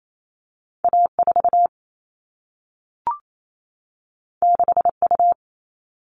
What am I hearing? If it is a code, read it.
Morse code, heard as a keyed tone, about 28 wpm, 700 Hz. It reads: A4 6U
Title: All Cut Numbers - 2-Digit 28wpm